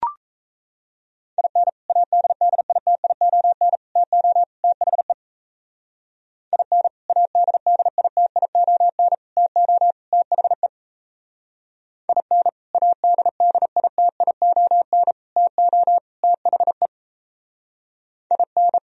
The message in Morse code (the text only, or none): IN ADDITION TO THE IN ADDITION TO THE IN ADDITION TO THE IN